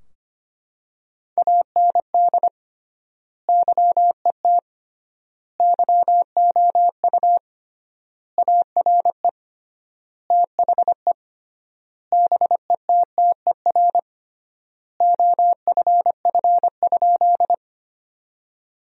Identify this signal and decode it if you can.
Morse code, heard as a keyed tone, about 25 wpm, 700 Hz. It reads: AND YET YOU ARE THE BETTER OFF?
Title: 3x Word-Spacing - Sentences from Top 500 Words - 25wpm